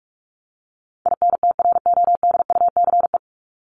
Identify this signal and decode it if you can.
Morse code, heard as a keyed tone, about 44 wpm, 700 Hz. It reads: INTRODUCE
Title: ED Suffix - Top 100 Words with Base Words 45wpm